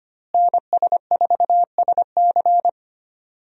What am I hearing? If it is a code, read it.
Morse code, heard as a keyed tone, about 25 wpm, 700 Hz. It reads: NS4SC
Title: USA Call Signs - 5 Characters - Repeated 2x - 25wpm